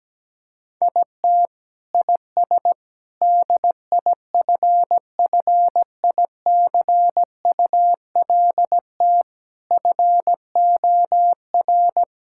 Morse code, heard as a keyed tone, about 17 wpm, 700 Hz. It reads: IT IS DIFFICULT FOR
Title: Sentences from Top 1000 Words - Review Part 6 - 17wpm